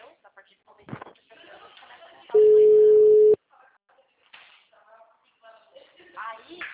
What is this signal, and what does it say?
Ringback tone (Europe, 425 Hz), with speech.
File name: out- -2005- - - .44923